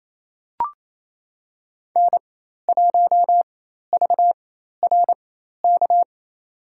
Morse code, heard as a keyed tone, about 28 wpm, spaced as for 13 wpm, 700 Hz. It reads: N1VRK
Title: USA Call Signs - 4x Character Spacing - 28wpm